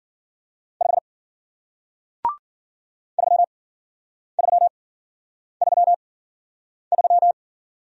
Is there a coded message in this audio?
Yes: Morse code from 0.8 to 7.3 s, heard as a keyed tone, about 51 wpm, spaced as for 9 wpm, 700 Hz. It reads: 5 3333